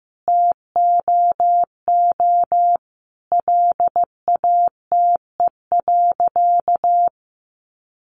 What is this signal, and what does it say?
Morse code, heard as a keyed tone, about 15 wpm, 700 Hz. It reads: TOO LATE.
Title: Sentences from Top 300 Words 15wpm